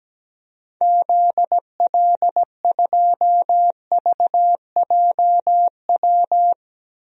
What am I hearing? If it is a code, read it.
Morse code, heard as a keyed tone, about 17 wpm, 700 Hz. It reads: ZL2VJW
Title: DXCC-100 Call Signs - 6 Characters - Repeated 3x - 17wpm